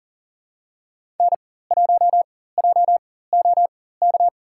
Morse code, heard as a keyed tone, about 40 wpm, spaced as for 19 wpm, 700 Hz. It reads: N1JOK
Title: USA Call Signs - 5 Characters - 4x Character Spacing - 40wpm